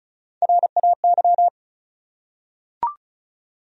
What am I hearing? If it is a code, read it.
Morse code, heard as a keyed tone, about 34 wpm, 700 Hz. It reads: RAY
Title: USA Common Names - Max 4 Letters - 35wpm